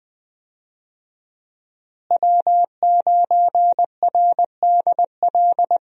Morse code, heard as a keyed tone, about 20 wpm, 700 Hz. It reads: W9RDL
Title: USA Call Signs - 5 Characters - Repeated 2x - 20wpm